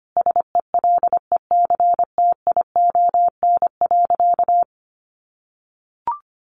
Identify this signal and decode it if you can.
Morse code, heard as a keyed tone, about 25 wpm, 700 Hz. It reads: SELECTION.
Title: Sentences from Top 1400 Words 25wpm